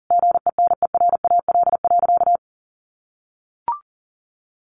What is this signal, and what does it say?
Morse code, heard as a keyed tone, about 40 wpm, 700 Hz. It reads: GENERAL.
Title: Sentences from Top 1500 Words 40wpm